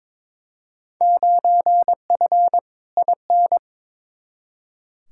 Morse code, heard as a keyed tone, about 22 wpm, 700 Hz. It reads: 9F IN
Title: Field Day Exchanges Encore - 22wpm